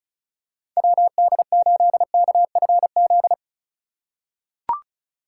Morse code, heard as a keyed tone, about 34 wpm, 700 Hz. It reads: WD8KFZ
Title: USA Call Signs - 6 Characters - 2x Character Spacing - 35wpm